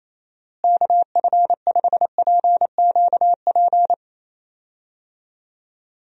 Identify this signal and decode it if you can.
Morse code, heard as a keyed tone, about 28 wpm, 700 Hz. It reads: KF5PQP
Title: USA Call Signs - 6 Characters - Repeated 2x - 28wpm